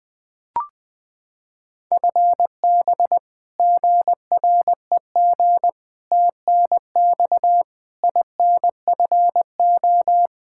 Morse code, heard as a keyed tone, about 20 wpm, 700 Hz. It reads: FB GREG TNX INFO
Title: Practice QSO - Exchange 3 - More Info - 20wpm